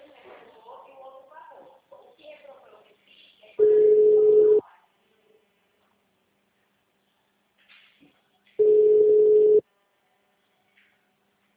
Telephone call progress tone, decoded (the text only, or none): ringback tone